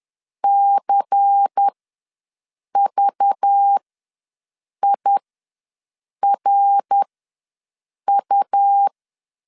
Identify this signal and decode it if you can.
Morse code, heard as a keyed tone, about 10 wpm, spaced as for 6 wpm, 800 Hz. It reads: CVIRU
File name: LEZIONI / 7 GF9Z / 7 G ws 10wpm 3843